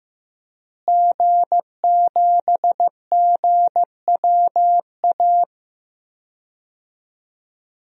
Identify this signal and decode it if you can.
Morse code, heard as a keyed tone, about 15 wpm, 700 Hz. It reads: G7GWA
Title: DXCC-100 Call Signs - 5 Characters - Repeated 3x - 15wpm